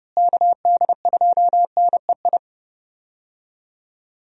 Morse code, heard as a keyed tone, about 30 wpm, 700 Hz. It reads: KD2NEI
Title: USA Call Signs - 6 Characters - Repeated 2x - 30wpm